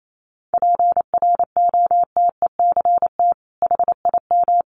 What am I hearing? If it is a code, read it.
Morse code, heard as a keyed tone, about 28 wpm, 700 Hz. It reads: PROTECT HIM
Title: Sentences from Top 900 Words 28wpm